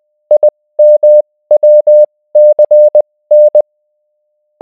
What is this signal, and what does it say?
Morse code, heard as a keyed tone, about 20 wpm, spaced as for 16 wpm, 600 Hz. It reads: IMWCN